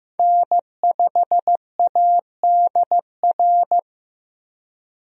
Morse code, heard as a keyed tone, about 15 wpm, 700 Hz. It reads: N5ADR
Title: USA Call Signs - 5 Characters - 15wpm